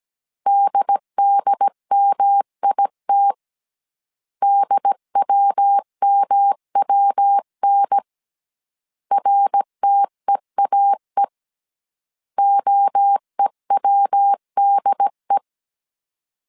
Morse code, heard as a keyed tone, about 17 wpm, 800 Hz. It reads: DDMIT DWMWN RTEAE OEWDE
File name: CW / DET / 1 3 1500C S C 16wpm 2148